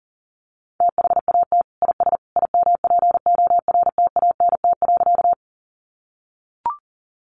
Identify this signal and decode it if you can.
Morse code, heard as a keyed tone, about 40 wpm, 700 Hz. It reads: THAT IS IMPORTANT.
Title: Sentences from Top 600 Words 40wpm